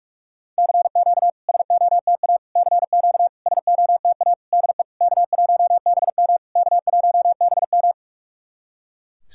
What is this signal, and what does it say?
Morse code, heard as a keyed tone, about 46 wpm, 700 Hz. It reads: CQ SOTA CQ SOTA DE K1BM K1BM
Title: SOTA QSO Practice 45wpm